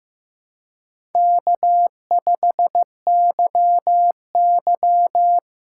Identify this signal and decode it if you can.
Morse code, heard as a keyed tone, about 15 wpm, 700 Hz. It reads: K5YY
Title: USA Call Signs - 4 Characters - 6x Character Spacing - 15wpm